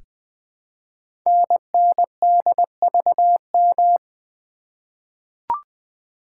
Morse code, heard as a keyed tone, about 20 wpm, 700 Hz. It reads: NNDVM
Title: Code Groups - 5 Letters - 20wpm